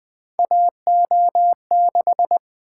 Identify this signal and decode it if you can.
Morse code, heard as a keyed tone, about 20 wpm, 700 Hz. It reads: AO6
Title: Three Letter-Number Combinations - 20wpm